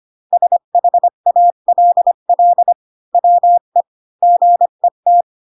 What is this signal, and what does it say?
Morse code, heard as a keyed tone, about 25 wpm, 700 Hz. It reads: SHALL WE GET
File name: Pinocchio / Pinocchio ch8 20 25 ik1hge